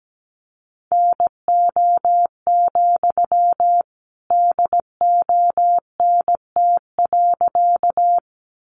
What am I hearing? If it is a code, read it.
Morse code, heard as a keyed tone, about 17 wpm, 700 Hz. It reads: NO, DONT.
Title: Sentences from Top 300 Words 17wpm